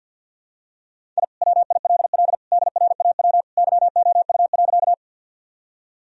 Morse code, heard as a keyed tone, about 52 wpm, 700 Hz. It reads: I WILL DRAW YOU.
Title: Sentences from Top 1000 Words - Review Part 1 - 50wpm